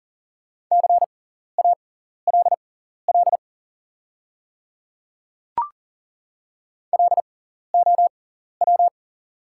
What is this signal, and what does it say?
Morse code, heard as a keyed tone, about 40 wpm, spaced as for 14 wpm, 690 Hz. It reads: CALL LOW
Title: 6x Character Spacing - Top 200 Words - 40wpm